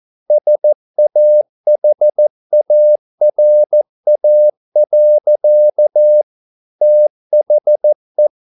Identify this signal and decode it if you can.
Morse code, heard as a keyed tone, about 14 wpm, 600 Hz. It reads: SAHARA. THE